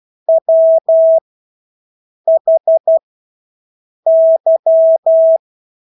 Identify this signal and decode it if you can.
Morse code, heard as a keyed tone, about 12 wpm, spaced as for 6 wpm, 650 Hz. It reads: WHY
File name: Engelske ord 12-6 wpm